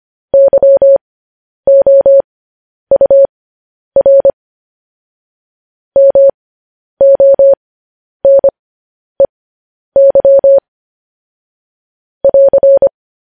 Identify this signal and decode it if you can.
Morse code, heard as a keyed tone, about 25 wpm, spaced as for 10 wpm, 550 Hz. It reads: YOUR MONEY <AR>
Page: news-headlines-in-morse-code-at-10-wpm